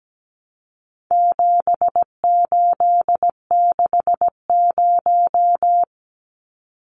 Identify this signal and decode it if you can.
Morse code, heard as a keyed tone, about 17 wpm, 700 Hz. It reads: 7860
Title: Numbers - 4-Digit 17wpm